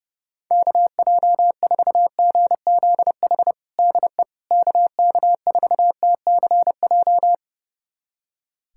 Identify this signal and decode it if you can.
Morse code, heard as a keyed tone, about 28 wpm, 700 Hz. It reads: KJ4GZH DE KK4TCJ
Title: USA Callsign DE Callsign - 6 Characters - 30wpm